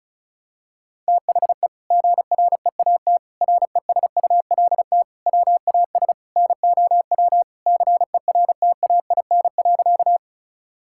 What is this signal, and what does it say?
Morse code, heard as a keyed tone, about 36 wpm, 700 Hz. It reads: THE GREAT RESULT WAS NOW CERTAIN.
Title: Sentences from Top 1000 Words - Review Part 5 - 35wpm